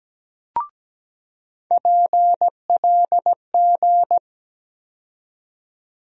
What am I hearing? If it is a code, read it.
Morse code, heard as a keyed tone, about 17 wpm, 700 Hz. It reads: PLG
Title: USA Call Sign Postfixes - 3 Letters 17wpm